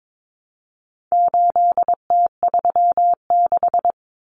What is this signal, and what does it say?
Morse code, heard as a keyed tone, about 22 wpm, 700 Hz. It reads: 8T36
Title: Common Cut Numbers - 4-Digit 22wpm